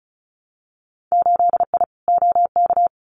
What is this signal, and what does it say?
Morse code, heard as a keyed tone, about 34 wpm, 700 Hz. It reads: 8I OK
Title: Winter Field Day Exchanges Encore 35wpm